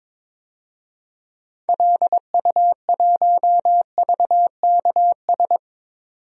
Morse code, heard as a keyed tone, about 22 wpm, 700 Hz. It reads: LU1VKS
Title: DXCC-100 Call Signs - 6 Characters - 22wpm